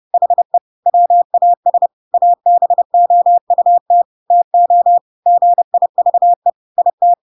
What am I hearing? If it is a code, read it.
Morse code, heard as a keyed tone, about 30 wpm, 700 Hz. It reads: HE WAS ABOUT TO GIVE IT
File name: Pinocchio / Pinocchio ch1 30 30 ik1hge